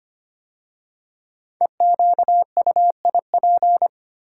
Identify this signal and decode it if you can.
Morse code, heard as a keyed tone, about 25 wpm, 700 Hz. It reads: EQUIP
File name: 5 Letter Words - First 3 Letters Given - 25wpm